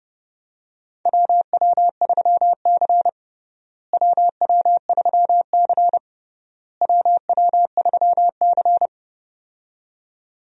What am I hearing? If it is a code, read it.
Morse code, heard as a keyed tone, about 29 wpm, 700 Hz. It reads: WW3C WW3C WW3C
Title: USA Call Signs - 4 Characters - Repeated 3x - 30wpm